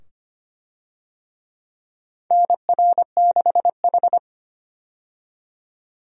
Morse code, heard as a keyed tone, about 25 wpm, 700 Hz. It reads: NR6H
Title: USA Call Signs - 4 Characters - 25wpm